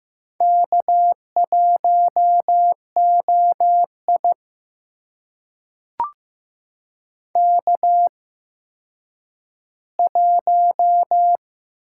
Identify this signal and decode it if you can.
Morse code, heard as a keyed tone, about 16 wpm, spaced as for 9 wpm, 700 Hz. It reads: K1OI K1
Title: USA Call Signs - 8x Character Spacing - 15wpm